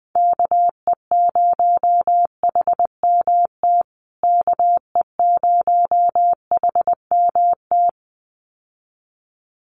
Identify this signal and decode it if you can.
Morse code, heard as a keyed tone, about 19 wpm, 700 Hz. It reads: KE0HMT KE0HMT
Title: USA Call Signs Repeated 2x 20wpm